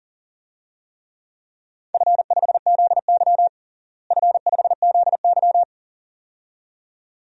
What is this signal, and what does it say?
Morse code, heard as a keyed tone, about 37 wpm, 700 Hz. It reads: F5ZY F5ZY
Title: DXCC-100 Call Signs - 4 Characters - Repeated 2x - 40wpm